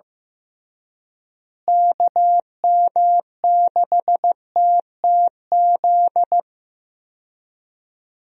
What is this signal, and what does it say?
Morse code, heard as a keyed tone, about 15 wpm, 700 Hz. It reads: KM6TTZ
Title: USA Call Signs - 6 Characters - Repeated 2x - 15wpm